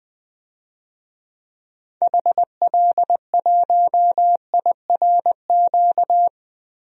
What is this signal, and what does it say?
Morse code, heard as a keyed tone, about 20 wpm, 700 Hz. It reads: HL1IRQ
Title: DXCC-100 Call Signs - 6 Characters - 20wpm